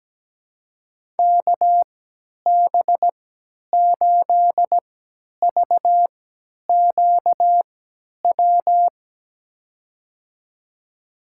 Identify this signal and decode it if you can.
Morse code, heard as a keyed tone, about 17 wpm, spaced as for 10 wpm, 700 Hz. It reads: KB8VQW
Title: USA Call Signs - 6 Characters - 3x Character Spacing - 17wpm